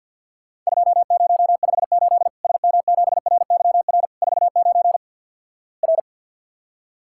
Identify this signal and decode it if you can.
Morse code, heard as a keyed tone, about 49 wpm, 700 Hz. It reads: 2058 SM7RYR 49 R
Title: Worked All Europe DX Contest - QTC Exchange - 50wpm